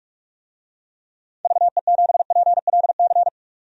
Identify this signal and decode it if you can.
Morse code, heard as a keyed tone, about 46 wpm, 700 Hz. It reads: VE7PLC
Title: DXCC-100 Call Signs - 6 Characters - 45wpm